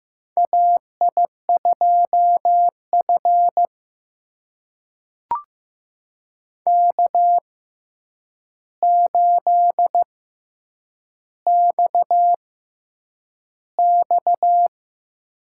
Morse code, heard as a keyed tone, about 15 wpm, spaced as for 8 wpm, 700 Hz. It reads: AI2F K8XX
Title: USA Call Signs - 4 Characters - 6x Character Spacing - 15wpm